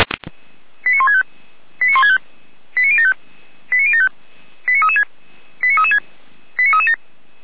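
Five-tone selective call (ZVEI), heard as quick runs of five tones, tones of about 80 ms. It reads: 89165 89165 89075 89075 89336 89336 89337
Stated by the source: Five-tone ZVEI selective call received on an SDR.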